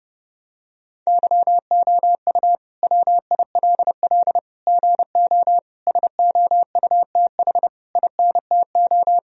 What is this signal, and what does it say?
Morse code, heard as a keyed tone, about 30 wpm, 700 Hz. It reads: YOU WILL GO SOUTH INTO
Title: Sentences from Top 1000 Words - Review Part 2 - 30wpm